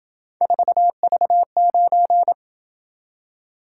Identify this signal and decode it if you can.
Morse code, heard as a keyed tone, about 26 wpm, 700 Hz. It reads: 4V9